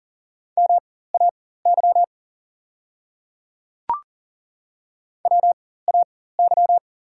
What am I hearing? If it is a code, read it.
Morse code, heard as a keyed tone, about 38 wpm, spaced as for 18 wpm, 690 Hz. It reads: MAY WAY
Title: 4x Character Spacing - Top 200 Words - 40wpm